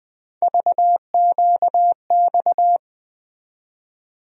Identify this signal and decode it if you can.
Morse code, heard as a keyed tone, about 21 wpm, 700 Hz. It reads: VQX
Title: USA Call Sign Postfixes - 3 Letters 20wpm